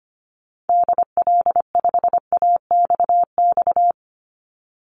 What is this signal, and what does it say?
Morse code, heard as a keyed tone, about 25 wpm, 700 Hz. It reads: DL5AXX
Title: Call Signs - 25wpm